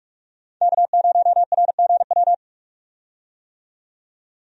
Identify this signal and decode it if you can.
Morse code, heard as a keyed tone, about 46 wpm, 700 Hz. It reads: K0RGW
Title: USA Call Signs - 5 Characters - 45wpm